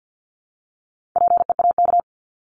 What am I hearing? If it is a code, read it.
Morse code, heard as a keyed tone, about 48 wpm, 700 Hz. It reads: PEAK
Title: Top 1751-2000 Words 50wpm